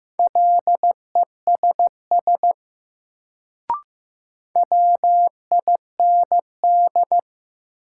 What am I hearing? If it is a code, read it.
Morse code, heard as a keyed tone, about 15 wpm, 700 Hz. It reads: LESS WIND
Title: Sentences from Top 400 Words - New Word Drill - 15wpm